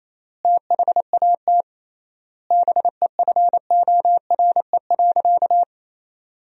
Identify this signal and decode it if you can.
Morse code, heard as a keyed tone, about 28 wpm, 700 Hz. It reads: THAT BEFORE.
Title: 3x Word-Spacing - Sentences from Top 400 Words - 28wpm